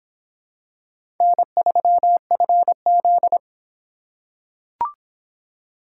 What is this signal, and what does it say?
Morse code, heard as a keyed tone, about 26 wpm, 700 Hz. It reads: N3FZ